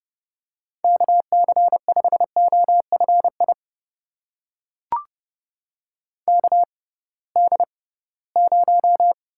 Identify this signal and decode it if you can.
Morse code, heard as a keyed tone, about 30 wpm, spaced as for 19 wpm, 700 Hz. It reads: KC5OFI KD0
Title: USA Call Signs - 6 Characters - 6x Character Spacing - 30wpm